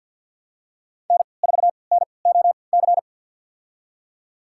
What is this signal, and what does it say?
Morse code, heard as a keyed tone, about 52 wpm, spaced as for 29 wpm, 700 Hz. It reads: N4NOC
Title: USA Call Signs - 5 Characters - 3x Character Spacing - 50wpm